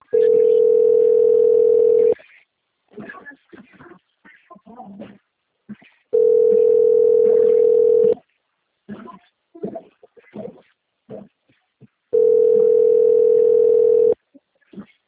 Ringback tone (North America), 440 + 480 Hz.
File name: out- - - - - .4793